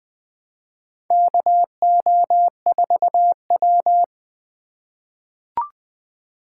Morse code, heard as a keyed tone, about 20 wpm, 700 Hz. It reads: KO4W
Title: Four Letter-Number Combinations - 20wpm